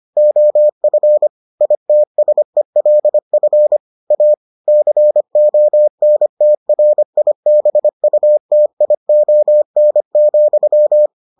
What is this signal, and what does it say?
Morse code, heard as a keyed tone, about 25 wpm, 600 Hz. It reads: OF ITSELF A CONTRIBUTION,